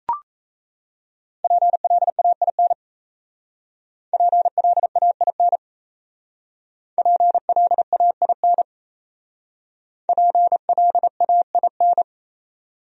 Morse code, heard as a keyed tone, about 36 wpm, 700 Hz. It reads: PLAIN PLAIN PLAIN PLAIN